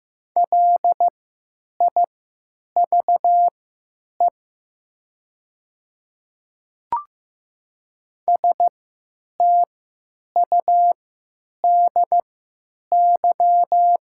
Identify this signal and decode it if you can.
Morse code, heard as a keyed tone, about 15 wpm, spaced as for 9 wpm, 700 Hz. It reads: LIVE STUDY
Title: 3x Character Spacing - Top 500 Words - 15wpm